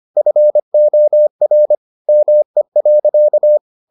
Morse code, heard as a keyed tone, about 25 wpm, 600 Hz. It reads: FOR ME.